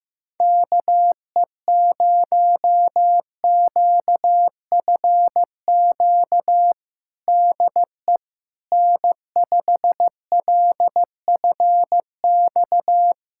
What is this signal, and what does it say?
Morse code, heard as a keyed tone, about 15 wpm, 700 Hz. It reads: KE0QFQ DE N5LFX
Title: USA Callsign DE Callsign 15wpm